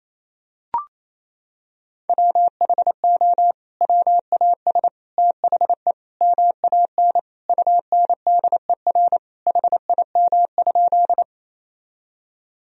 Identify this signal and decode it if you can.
Morse code, heard as a keyed tone, about 28 wpm, 700 Hz. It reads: WHO WAS THE MAN UNDER HIM?
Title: Sentences from Top 1000 Words - Review Part 1 - 28wpm